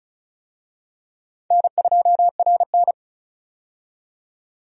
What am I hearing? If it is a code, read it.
Morse code, heard as a keyed tone, about 36 wpm, 700 Hz. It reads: N2RN
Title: USA Call Signs - 4 Characters - 35wpm